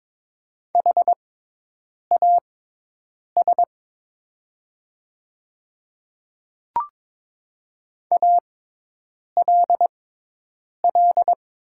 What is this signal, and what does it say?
Morse code, heard as a keyed tone, about 22 wpm, spaced as for 8 wpm, 690 Hz. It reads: HAS ALL
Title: 6x Character Spacing - Top 100 Words - 22wpm